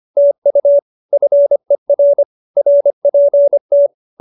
Morse code, heard as a keyed tone, about 25 wpm, 570 Hz. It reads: TU FER RPT